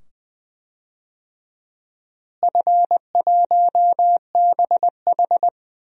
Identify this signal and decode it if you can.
Morse code, heard as a keyed tone, about 20 wpm, 700 Hz. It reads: F1BH
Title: DXCC-100 Call Signs - 4 Characters - Repeated 2x - 20wpm